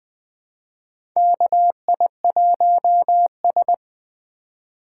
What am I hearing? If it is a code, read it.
Morse code, heard as a keyed tone, about 20 wpm, 700 Hz. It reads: KI1S
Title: USA Call Signs - 3x Character Spacing - 20wpm